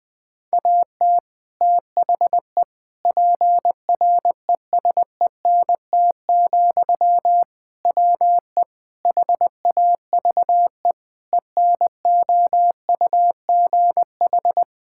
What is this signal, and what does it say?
Morse code, heard as a keyed tone, about 20 wpm, 700 Hz. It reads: AT THE PRESENT, WE HAVE ENOUGH
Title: Sentences from Top 1000 Words - Review Part 5 - 20wpm